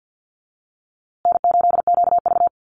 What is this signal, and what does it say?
Morse code, heard as a keyed tone, about 49 wpm, 700 Hz. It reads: N8QV
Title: USA Call Signs Repeated 2x 50wpm